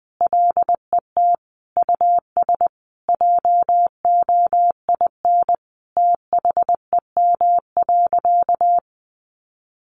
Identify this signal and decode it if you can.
Morse code, heard as a keyed tone, about 20 wpm, 700 Hz. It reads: LET US JOIN THEM.
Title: Sentences from Top 1400 Words 20wpm